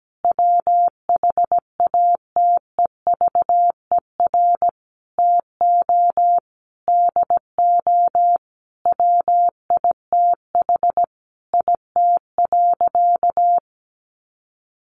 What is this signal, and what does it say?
Morse code, heard as a keyed tone, about 17 wpm, 700 Hz. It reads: WHATEVER TO DO WITH IT.
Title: Sentences from Top 1100 Words 17wpm